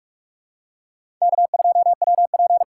Morse code, heard as a keyed tone, about 41 wpm, 700 Hz. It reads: K2WP
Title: USA Call Signs - 6x Character Spacing - 45wpm